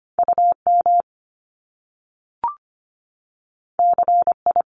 Morse code, heard as a keyed tone, about 25 wpm, 700 Hz. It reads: UM CI
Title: Top Two Letter Combinations in Words 25wpm